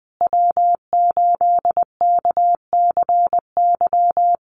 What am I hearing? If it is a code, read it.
Morse code, heard as a keyed tone, about 20 wpm, 700 Hz. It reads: W8KCY
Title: USA Call Signs Repeated 3x 20wpm